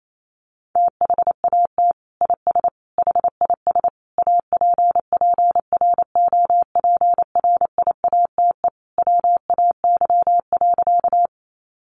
Morse code, heard as a keyed tone, about 28 wpm, 690 Hz. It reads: THAT IS HIS APPROPRIATE WAY.
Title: Sentences from Top 900 Words 28wpm